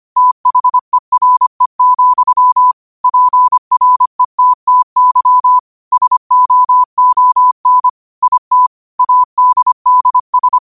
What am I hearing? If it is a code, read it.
Morse code, heard as a keyed tone, about 25 wpm, 1000 Hz. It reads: THERE, PRETTY SOON IT ADDS